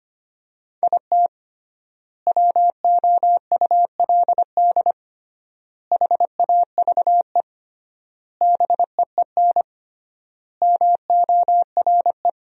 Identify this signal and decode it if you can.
Morse code, heard as a keyed tone, about 25 wpm, 700 Hz. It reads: IT WOULD HAVE BEEN MORE
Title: 3x Word-Spacing - Sentences from Top 400 Words - 25wpm